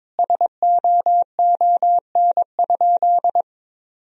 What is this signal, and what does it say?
Morse code, heard as a keyed tone, about 22 wpm, 700 Hz. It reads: SOON?
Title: Sentences from Top 400 Words and Abbreviations 22wpm